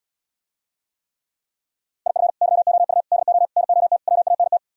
Morse code, heard as a keyed tone, about 77 wpm, 700 Hz. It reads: IF YOU DO NOT WANT